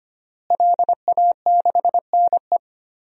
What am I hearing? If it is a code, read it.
Morse code, heard as a keyed tone, about 25 wpm, 700 Hz. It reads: LA6NE
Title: DXCC-100 Call Signs - 5 Characters - Repeated 3x - 25wpm